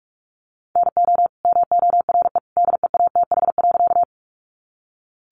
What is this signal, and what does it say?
Morse code, heard as a keyed tone, about 44 wpm, 700 Hz. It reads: NO MORE DEATH.
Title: Sentences from Top 1100 Words 45wpm